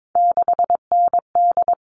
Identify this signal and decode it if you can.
Morse code, heard as a keyed tone, about 22 wpm, 700 Hz. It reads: 6ND
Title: All Cut Numbers - 3-Digit 22wpm